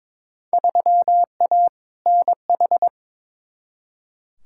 Morse code, heard as a keyed tone, about 22 wpm, 700 Hz. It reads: 3A NH
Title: Field Day Exchanges Encore - 22wpm